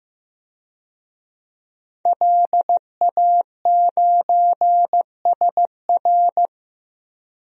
Morse code, heard as a keyed tone, about 15 wpm, 700 Hz. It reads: LA9SR
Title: DXCC-100 Call Signs - 5 Characters - Repeated 3x - 15wpm